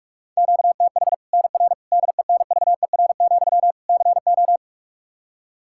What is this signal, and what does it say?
Morse code, heard as a keyed tone, about 41 wpm, 700 Hz. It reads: QTH NR DENVER, CO
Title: K1USN SST QSO Practice 45wpm